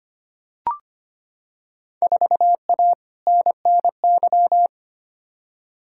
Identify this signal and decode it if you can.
Morse code, heard as a keyed tone, about 25 wpm, 700 Hz. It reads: 4A NNY
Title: Field Day Exchanges Encore - 25wpm